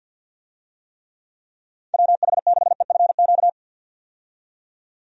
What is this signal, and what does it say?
Morse code, heard as a keyed tone, about 48 wpm, 700 Hz. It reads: WH6EFQ
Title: USA Call Signs - 6 Characters - 50wpm